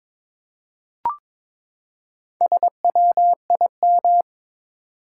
Morse code, heard as a keyed tone, about 22 wpm, 700 Hz. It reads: SWIM
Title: Four Letter Words - 22wpm